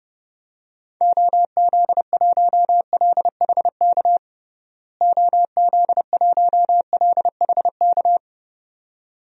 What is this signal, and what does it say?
Morse code, heard as a keyed tone, about 30 wpm, 700 Hz. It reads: OZ1LHK OZ1LHK
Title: DXCC-100 Call Signs - 6 Characters - Repeated 2x - 30wpm